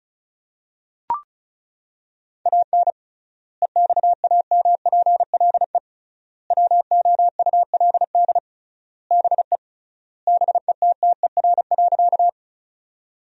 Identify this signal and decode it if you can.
Morse code, heard as a keyed tone, about 36 wpm, 700 Hz. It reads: AN EXAMPLE WOULD BE BETTER.
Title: 3x Word-Spacing - Sentences from Top 400 Words - 35wpm